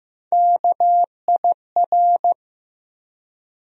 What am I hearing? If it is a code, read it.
Morse code, heard as a keyed tone, about 16 wpm, 700 Hz. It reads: KIR
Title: USA Call Sign Postfixes - 3 Letters 15wpm